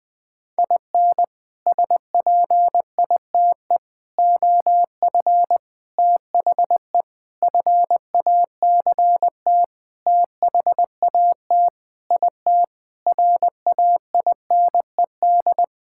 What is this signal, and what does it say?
Morse code, heard as a keyed tone, about 20 wpm, 700 Hz. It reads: IN SPITE OF THE FACT THAT IT RAINED
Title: Sentences with 6-Word Phrases - 20wpm